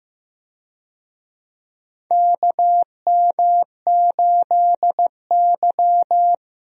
Morse code, heard as a keyed tone, about 15 wpm, 700 Hz. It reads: KM8Y
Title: USA Call Signs - 4 Characters - Repeated 2x - 15wpm